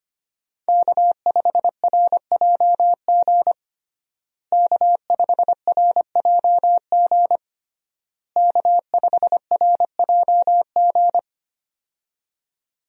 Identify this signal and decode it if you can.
Morse code, heard as a keyed tone, about 25 wpm, 700 Hz. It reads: K5RJG K5RJG K5RJG
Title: USA Call Signs - 5 Characters - Repeated 3x - 25wpm